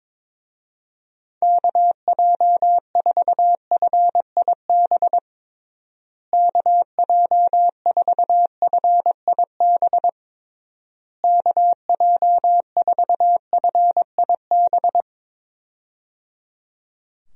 Morse code, heard as a keyed tone, about 22 wpm, 700 Hz. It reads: KJ4FIB KJ4FIB KJ4FIB
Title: USA Call Signs - 6 Characters - Repeated 3x - 22wpm